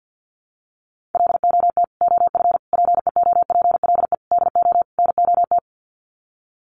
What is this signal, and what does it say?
Morse code, heard as a keyed tone, about 50 wpm, 700 Hz. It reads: LOT OF PEOPLE DO NOT
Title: Sets of 5 Words Encore 50wpm